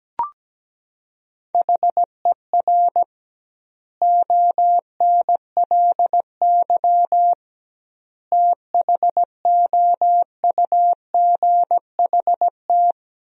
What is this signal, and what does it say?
Morse code, heard as a keyed tone, about 17 wpm, 700 Hz. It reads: HER ONLY THOUGHT
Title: 2x Word-Spacing - Sentences from Top 400 Words - 17wpm